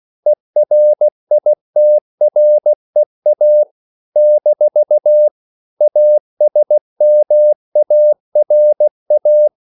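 Morse code, heard as a keyed tone, about 16 wpm, 600 Hz. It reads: ERITREA - ASMARA
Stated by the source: CW practice of national capitals at 16 words per minute, from ON6ZQ.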